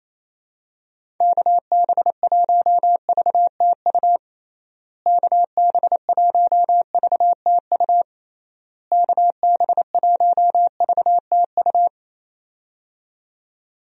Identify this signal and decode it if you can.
Morse code, heard as a keyed tone, about 28 wpm, 700 Hz. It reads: KB1VTU KB1VTU KB1VTU
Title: USA Call Signs - 6 Characters - Repeated 3x - 28wpm